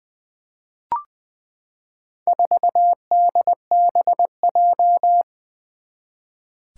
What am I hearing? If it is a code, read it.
Morse code, heard as a keyed tone, about 20 wpm, 700 Hz. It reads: 4DBJ
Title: Four Challenging Letter-Number Combinations 20wpm